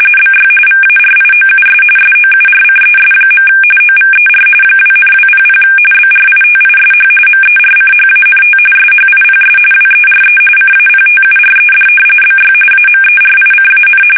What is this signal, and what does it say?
RTTY, about 45.45 baud. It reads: BY VARIOUS SOURCES. OVER 2,200 BRITISH MERCHANT SHIPS WERE CAPTURED BY AMERICAN
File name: fsk850 / usnavy1